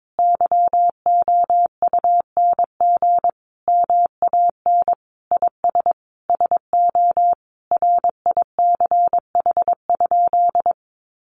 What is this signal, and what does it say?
Morse code, heard as a keyed tone, about 22 wpm, 700 Hz. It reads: YOUNG MAN IS SO RICH?
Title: Sentences from Top 1100 Words 22wpm